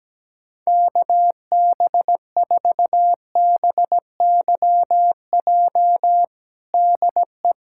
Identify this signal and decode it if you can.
Morse code, heard as a keyed tone, about 17 wpm, 700 Hz. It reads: KB4BYJ DE
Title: USA Callsign DE Callsign 17wpm